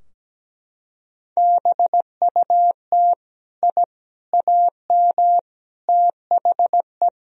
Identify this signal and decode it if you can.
Morse code, heard as a keyed tone, about 17 wpm, 700 Hz. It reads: BUT I AM THE
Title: Sentences from Top 1000 Words - Review Part 4 - 17wpm